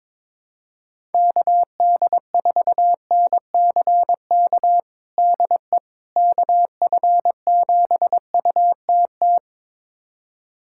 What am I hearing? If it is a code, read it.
Morse code, heard as a keyed tone, about 22 wpm, 700 Hz. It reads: KD4NCK DE KF7UTT
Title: USA Callsign DE Callsign 22wpm